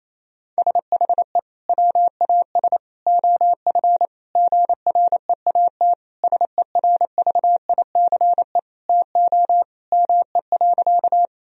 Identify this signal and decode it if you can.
Morse code, heard as a keyed tone, about 28 wpm, 700 Hz. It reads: SHE WAS OF GREAT SERVICE TO ME.
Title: Sentences from Top 1000 Words - Review Part 4 - 28wpm